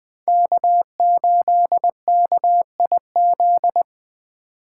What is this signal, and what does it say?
Morse code, heard as a keyed tone, about 20 wpm, 700 Hz. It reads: K8KIZ